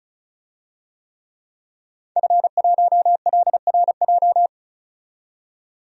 Morse code, heard as a keyed tone, about 36 wpm, 700 Hz. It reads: F1LRJ
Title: DXCC-100 Call Signs - 5 Characters - 35wpm